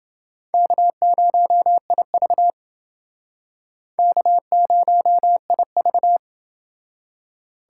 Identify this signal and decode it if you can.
Morse code, heard as a keyed tone, about 29 wpm, 700 Hz. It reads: K0IV K0IV